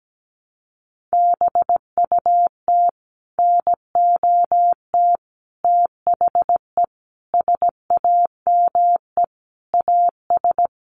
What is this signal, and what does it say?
Morse code, heard as a keyed tone, about 17 wpm, 700 Hz. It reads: BUT NOT THE SAME AS
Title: A Sound Of Thunder 17wpm